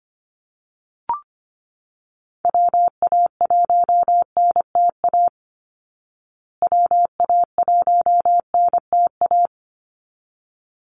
Morse code, heard as a keyed tone, about 25 wpm, 700 Hz. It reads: WA1NTA WA1NTA
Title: Call Signs Repeated 3x - 25wpm